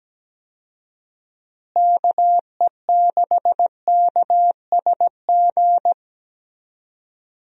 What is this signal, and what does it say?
Morse code, heard as a keyed tone, about 17 wpm, 700 Hz. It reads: KE6KSG
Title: USA Call Signs - 6 Characters - Repeated 2x - 17wpm